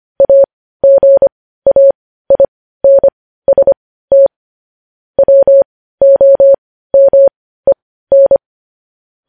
Morse code, heard as a keyed tone, about 25 wpm, spaced as for 15 wpm, 550 Hz. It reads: AGAINST WOMEN
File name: podcast / News headlines-1x-25x15